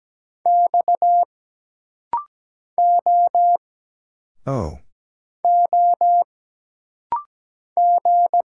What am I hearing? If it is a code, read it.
Morse code, heard as a keyed tone, about 17 wpm, 700 Hz. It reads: X O O G
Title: Single Letters - Rapid-Fire 17wpm